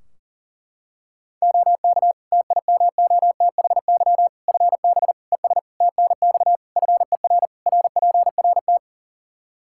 Morse code, heard as a keyed tone, about 40 wpm, 700 Hz. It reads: OK TIMOTHY FB ES TNX FER RPRT
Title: K1USN SST QSO Practice 40wpm